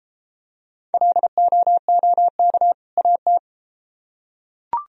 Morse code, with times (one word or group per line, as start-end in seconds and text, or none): 0.9-2.7 LOOK
3.0-3.4 AT